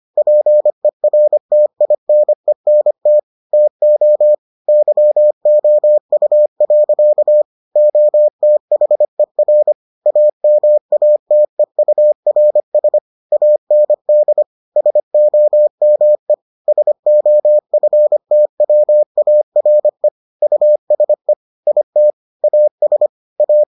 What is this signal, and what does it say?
Morse code, heard as a keyed tone, about 25 wpm, 600 Hz. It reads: PERTINENT TO YOU. OTHER AMATEURS AND SOME SOFTWARE USE IT AS A